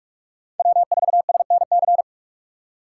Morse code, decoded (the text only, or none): W4SNC